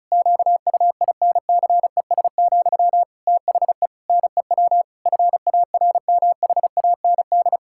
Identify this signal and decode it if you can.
Morse code, heard as a keyed tone, about 36 wpm, 700 Hz. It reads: QUINCES, THE NEW FARMHAND
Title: Pangram Challenge v2 - 35wpm